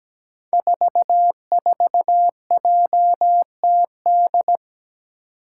Morse code, heard as a keyed tone, about 17 wpm, 700 Hz. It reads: 44JTD
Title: Five Challenging Letter-Number Combinations 17wpm